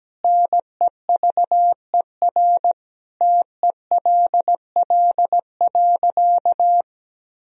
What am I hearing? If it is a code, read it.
Morse code, heard as a keyed tone, about 17 wpm, 700 Hz. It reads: NEVER TELL.
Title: Sentences from Top 300 Words and Abbreviations 17wpm